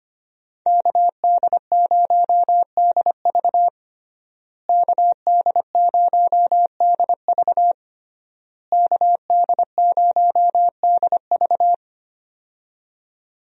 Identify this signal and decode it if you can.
Morse code, heard as a keyed tone, about 25 wpm, 700 Hz. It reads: KD0DV KD0DV KD0DV
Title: USA Call Signs - 5 Characters - Repeated 3x - 25wpm